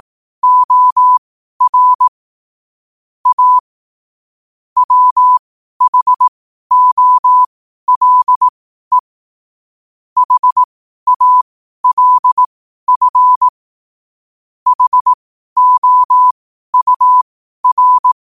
Morse code, heard as a keyed tone, about 18 wpm, spaced as for 12 wpm, 1000 Hz. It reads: OR A WHOLE HALF HOUR
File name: qotdfiles / 12wpm